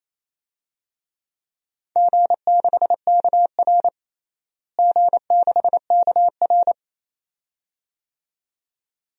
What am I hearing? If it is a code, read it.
Morse code, heard as a keyed tone, about 27 wpm, 700 Hz. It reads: G6KR G6KR
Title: DXCC-100 Call Signs - 4 Characters - Repeated 2x - 28wpm